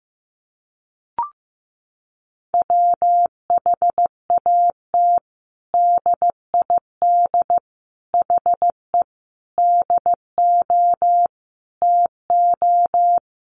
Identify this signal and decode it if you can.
Morse code, heard as a keyed tone, about 15 wpm, 700 Hz. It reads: WHAT DID HE DO TO
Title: Sentences from Top 100 Words 15wpm